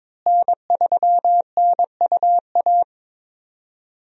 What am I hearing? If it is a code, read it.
Morse code, heard as a keyed tone, about 22 wpm, 700 Hz. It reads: N3NUA
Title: USA Call Signs - 5 Characters - Repeated 3x - 22wpm